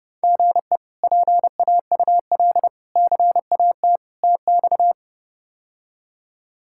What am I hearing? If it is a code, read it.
Morse code, heard as a keyed tone, about 32 wpm, 700 Hz. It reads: GE PAUL CAT TX
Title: K1USN SST QSO Practice - 30wpm